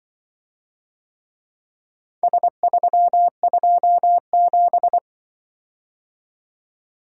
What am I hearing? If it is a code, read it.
Morse code, heard as a keyed tone, about 25 wpm, 700 Hz. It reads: S327
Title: Speed-Racing - Four Letter-Number Combinations - 20wpm